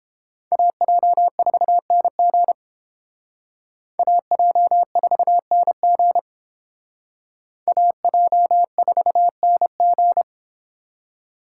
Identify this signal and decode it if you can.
Morse code, heard as a keyed tone, about 31 wpm, 700 Hz. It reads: AJ4NG AJ4NG AJ4NG